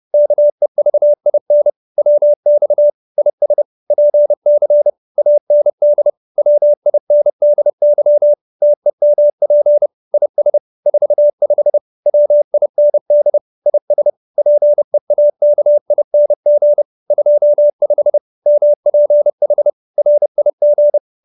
Morse code, heard as a keyed tone, about 30 wpm, 590 Hz. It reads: KEVIN WX IS PC AND WINDY TEMP IS 45 WIND IS PEAKING 25 MPH RIG